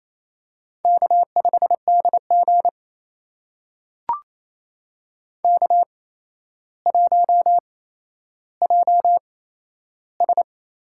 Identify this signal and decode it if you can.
Morse code, heard as a keyed tone, about 27 wpm, spaced as for 12 wpm, 700 Hz. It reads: K5DG K1JS
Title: USA Call Signs - 8x Character Spacing - 28wpm